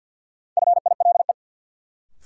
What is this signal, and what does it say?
Morse code, heard as a keyed tone, about 53 wpm, 700 Hz. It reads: FILE
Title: Sentences from Top 1000 Words - Review Part 4 - New Word Drill - 50wpm